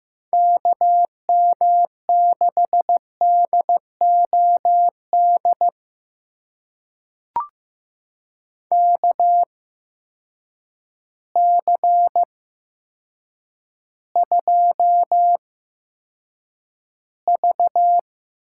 Morse code, heard as a keyed tone, about 15 wpm, spaced as for 8 wpm, 700 Hz. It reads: KM6DOD KC2V
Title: USA Call Signs - 6 Characters - 8x Character Spacing - 15wpm